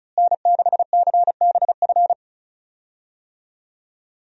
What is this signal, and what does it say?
Morse code, heard as a keyed tone, about 36 wpm, 700 Hz. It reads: N6CBF
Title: USA Call Signs - 5 Characters - Repeated 2x - 35wpm